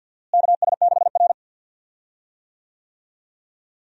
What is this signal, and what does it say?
Morse code, heard as a keyed tone, about 51 wpm, 700 Hz. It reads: KS6R
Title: USA Call Signs - 4 Characters - Repeated 2x - 50wpm